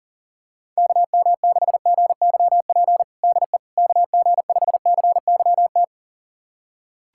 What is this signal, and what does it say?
Morse code, heard as a keyed tone, about 40 wpm, 700 Hz. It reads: KM6GYP DE KG5CYT
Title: USA Callsign DE Callsign - 6 Characters - 40wpm